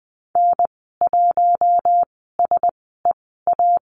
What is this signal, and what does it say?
Morse code, heard as a keyed tone, about 20 wpm, spaced as for 15 wpm, 700 Hz. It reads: N1SEA
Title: 042 Lesson 8g - Number 1 - Call Signs 20wpm